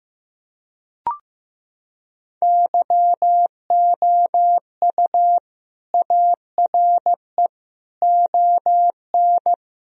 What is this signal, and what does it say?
Morse code, heard as a keyed tone, about 15 wpm, 700 Hz. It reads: YOU ARE ON
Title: Sentences from Top 1000 Words - Review Part 2 - 15wpm